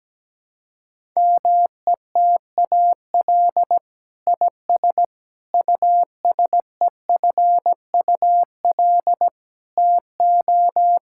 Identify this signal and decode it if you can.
Morse code, heard as a keyed tone, about 17 wpm, 700 Hz. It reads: METAL IS USEFUL TO
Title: Sentences from Top 1000 Words - Review Part 6 - 17wpm